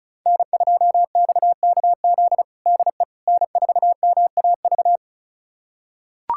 Morse code, heard as a keyed tone, about 34 wpm, 700 Hz. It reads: N2XKZ DE N4MAV
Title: USA Callsign DE Callsign - 5 Characters - 35wpm